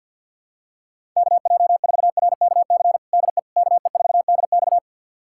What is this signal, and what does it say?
Morse code, heard as a keyed tone, about 52 wpm, 700 Hz. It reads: KJ4RKC DE KE4DX
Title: USA Callsign DE Callsign 50wpm